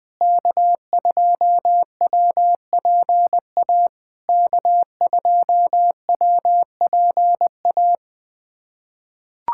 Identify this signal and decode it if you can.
Morse code, heard as a keyed tone, about 20 wpm, 700 Hz. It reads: K2WPA K2WPA
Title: POTA QSO Practice - 20wpm